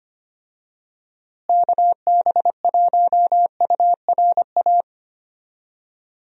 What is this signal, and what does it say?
Morse code, heard as a keyed tone, about 25 wpm, 700 Hz. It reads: KB1URA
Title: USA Call Signs - 6 Characters - Repeated 3x - 25wpm